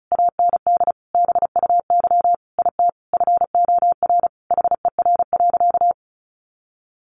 Morse code, heard as a keyed tone, about 34 wpm, 700 Hz. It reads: AND BUY IT FOR HER.
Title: Sentences from Top 1400 Words 35wpm